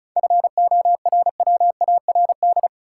Morse code, heard as a keyed tone, about 34 wpm, 700 Hz. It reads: FORWARD
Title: Sentences from Top 1300 Words - New Word Drill - 35wpm